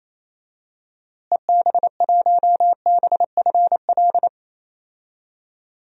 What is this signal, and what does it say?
Morse code, heard as a keyed tone, about 28 wpm, 700 Hz. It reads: EB1BFL
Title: DXCC-100 Call Signs - 6 Characters - Repeated 2x - 28wpm